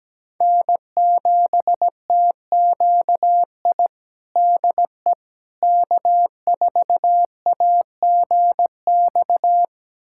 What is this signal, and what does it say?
Morse code, heard as a keyed tone, about 17 wpm, 700 Hz. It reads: N7TQI DE K4AGX
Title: USA Callsign DE Callsign 17wpm